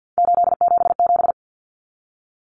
Morse code, heard as a keyed tone, about 48 wpm, 700 Hz. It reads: 877
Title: Common Cut Numbers - 3-Digit 50wpm